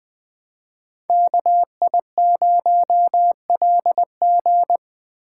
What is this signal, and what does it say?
Morse code, heard as a keyed tone, about 20 wpm, 700 Hz. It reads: KI0LG
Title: Speed Racing - USA 2x2 Call Signs 17wpm